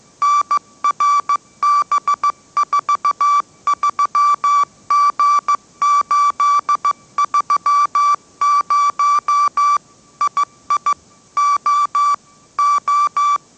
Morse code, heard as a keyed tone, about 15 wpm, 1160 Hz. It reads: NRB43G830 II O O
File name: mrx / mrx066 16wpm